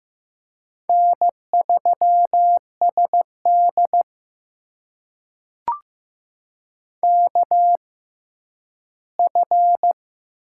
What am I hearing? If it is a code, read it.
Morse code, heard as a keyed tone, about 15 wpm, spaced as for 10 wpm, 700 Hz. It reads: N3SD KF
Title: USA Call Signs - 4 Characters - 6x Character Spacing - 15wpm